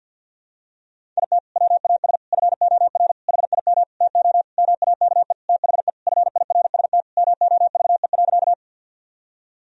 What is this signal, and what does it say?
Morse code, heard as a keyed tone, about 47 wpm, 700 Hz. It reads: IT WAS FOR HIM TO MAKE THE FIRST MOVE.
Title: Sentences from Top 1000 Words - Review Part 1 - 50wpm